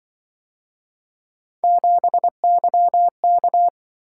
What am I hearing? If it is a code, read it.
Morse code, heard as a keyed tone, about 25 wpm, 700 Hz. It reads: 7YK
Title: Speed-Racing - Three Letter-Number Combinations - 20wpm